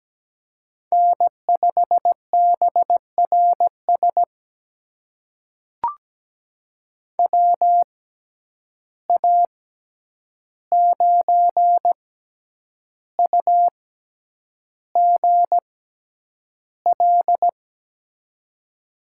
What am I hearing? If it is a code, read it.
Morse code, heard as a keyed tone, about 17 wpm, spaced as for 8 wpm, 700 Hz. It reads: N5BRS WA9UGL
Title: USA Call Signs - 6x Character Spacing - 17wpm